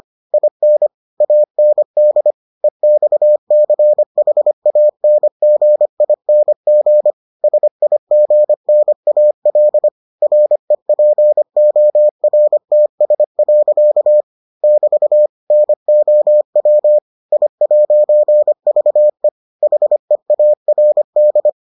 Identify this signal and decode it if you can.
Morse code, heard as a keyed tone, about 25 wpm, 600 Hz. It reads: IN AND EXCHANGING SIGNAL REPORTS. <BT> NOW I'VE HEARD